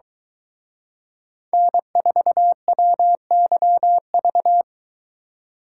Morse code, heard as a keyed tone, about 23 wpm, 700 Hz. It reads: N4WYV